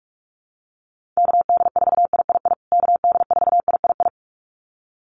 Morse code, heard as a keyed tone, about 44 wpm, 700 Hz. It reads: KD4III KD4III
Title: USA Call Signs Repeated 2x 45wpm